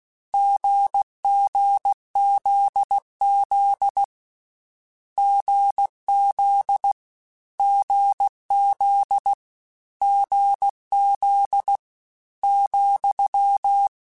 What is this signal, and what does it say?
Morse code, heard as a keyed tone, about 16 wpm, 800 Hz. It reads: GGZZ GZ GZ GZ ,
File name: Difficult Combos / Difficult Combos-mp3s / GZ GZ7 @15